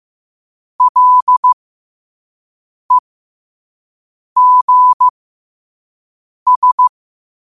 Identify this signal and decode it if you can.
Morse code, heard as a keyed tone, about 15 wpm, spaced as for 5 wpm, 1000 Hz. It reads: LEGS
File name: qotdfiles / 5wpm